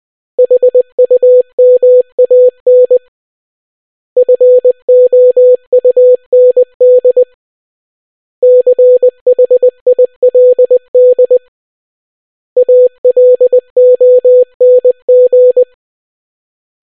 Morse code, heard as a keyed tone, about 20 wpm, 500 Hz. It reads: HUMAN FOUND CHILD ALONG